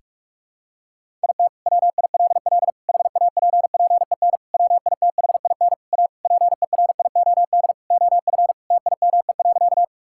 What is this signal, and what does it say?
Morse code, heard as a keyed tone, about 41 wpm, 700 Hz. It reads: IT WILL HAPPEN WITHIN A PERIOD OF TIME.
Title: Sentences from Top 1000 Words - Review Part 5 - 45wpm